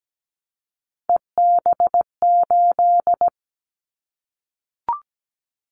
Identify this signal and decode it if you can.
Morse code, heard as a keyed tone, about 17 wpm, 700 Hz. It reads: EB8
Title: Callsign Prefixes for Top 100 DXCC - 17wpm